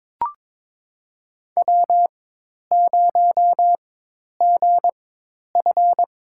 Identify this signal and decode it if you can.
Morse code, heard as a keyed tone, about 23 wpm, spaced as for 10 wpm, 700 Hz. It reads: W0GF
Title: USA Call Signs - 4 Characters - 4x Character Spacing - 22wpm